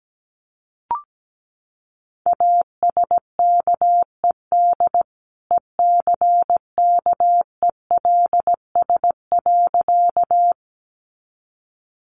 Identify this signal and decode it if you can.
Morse code, heard as a keyed tone, about 17 wpm, 700 Hz. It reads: ASKED ECKELS.
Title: A Sound Of Thunder 17wpm